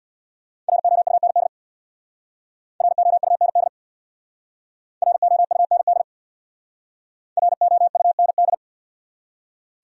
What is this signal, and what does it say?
Morse code, heard as a keyed tone, about 64 wpm, 700 Hz. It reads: ROUND ROUND ROUND ROUND